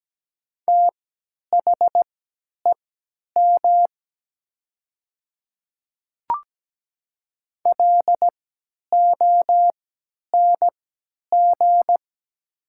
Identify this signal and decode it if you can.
Morse code, heard as a keyed tone, about 17 wpm, spaced as for 10 wpm, 700 Hz. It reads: THEM LONG
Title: 3x Character Spacing - Top 100 Words - 17wpm